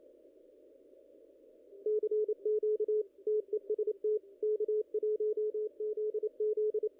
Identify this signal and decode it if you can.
Morse code, heard as a keyed tone, about 28 wpm, 410 Hz, with radio noise. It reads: CQ TEST K1ZZ
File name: contests / ss-cw-k5tr-op-n5rz -1800